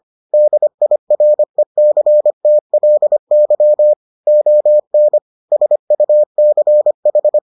Morse code, heard as a keyed tone, about 25 wpm, 600 Hz. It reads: DIRECTLY ON SUCH